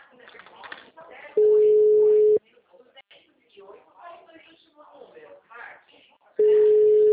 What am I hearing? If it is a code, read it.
Ringback tone (Europe, 425 Hz), with speech.